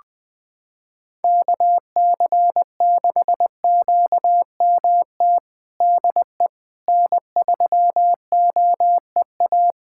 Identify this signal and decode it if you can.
Morse code, heard as a keyed tone, about 20 wpm, 700 Hz. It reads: KC6QMT DE N3OEA
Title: USA Callsign DE Callsign 20wpm